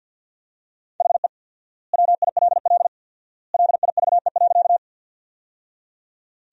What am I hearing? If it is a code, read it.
Morse code, heard as a keyed tone, about 50 wpm, 700 Hz. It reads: HE WILL LIVE.
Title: 4x Word-Spacing - Sentences from Top 200 Words - 50wpm